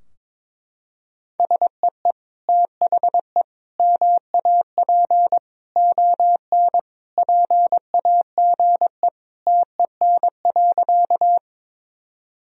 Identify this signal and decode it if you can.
Morse code, heard as a keyed tone, about 22 wpm, 700 Hz. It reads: SEE THE MAP ON PAGE TEN.
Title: Sentences from Top 1000 Words - Review Part 2 - 22wpm